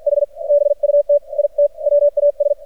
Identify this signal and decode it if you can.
Morse code, heard as a keyed tone, about 54 wpm, spaced as for 44 wpm, 580 Hz, with radio noise. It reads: H BUT IT WAS
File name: archived / CRSnet